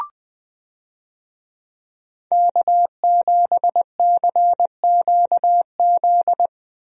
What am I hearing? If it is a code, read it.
Morse code, heard as a keyed tone, about 20 wpm, 700 Hz. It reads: K7CQZ
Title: USA Call Signs - 5 Characters - Repeated 3x - 20wpm